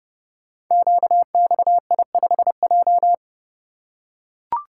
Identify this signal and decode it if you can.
Morse code, heard as a keyed tone, about 29 wpm, 700 Hz. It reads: QXI5J
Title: Code Groups - 5 Letter:Numbers - 30wpm